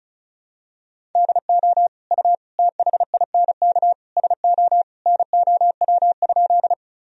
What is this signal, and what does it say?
Morse code, heard as a keyed tone, about 36 wpm, 700 Hz. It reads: DO U THINK SO NOW?
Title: Sentences from Top 200 Words and Abbreviations 35wpm